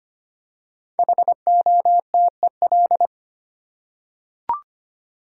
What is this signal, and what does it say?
Morse code, heard as a keyed tone, about 25 wpm, 700 Hz. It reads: HOTEL
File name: 5 Letter Words - First 3 Letters Given - 25wpm